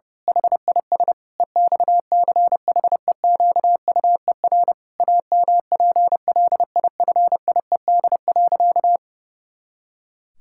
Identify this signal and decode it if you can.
Morse code, heard as a keyed tone, about 30 wpm, 700 Hz. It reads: HIS EXCHEQUER AMPLIFIED.
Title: Pangram Challenge v2 - 30wpm